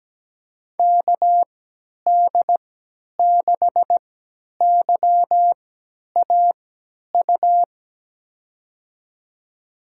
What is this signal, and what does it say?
Morse code, heard as a keyed tone, about 17 wpm, spaced as for 10 wpm, 700 Hz. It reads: KD6YAU
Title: USA Call Signs - 6 Characters - 3x Character Spacing - 17wpm